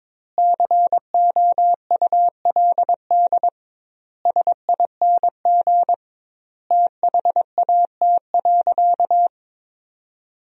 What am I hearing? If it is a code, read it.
Morse code, heard as a keyed tone, about 22 wpm, 700 Hz. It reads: COULD SING THAT.
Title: 2x Word-Spacing - Sentences from Top 500 Words - 22wpm